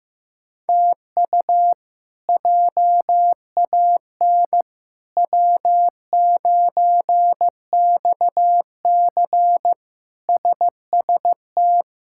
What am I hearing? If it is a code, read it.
Morse code, heard as a keyed tone, about 15 wpm, 700 Hz. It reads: TU JAN W9XC SST
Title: K1USN SST QSO Practice - 15wpm